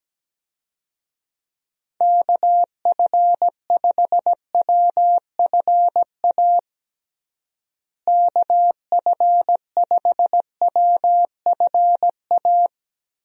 Morse code, heard as a keyed tone, about 17 wpm, 700 Hz. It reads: KF5WFA KF5WFA
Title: USA Call Signs - 6 Characters - Repeated 2x - 17wpm